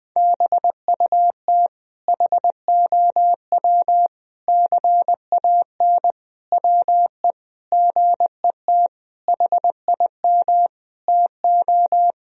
Morse code, heard as a keyed tone, about 21 wpm, 700 Hz. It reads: BUT HOW CAN WE GET HIM TO
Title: Sentences from Top 1000 Words - Review Part 4 - 20wpm